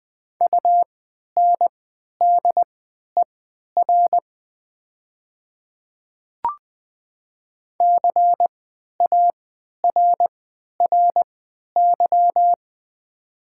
Morse code, heard as a keyed tone, about 20 wpm, spaced as for 11 wpm, 700 Hz. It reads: UNDER CARRY
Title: 3x Character Spacing - Top 300 Words - 20wpm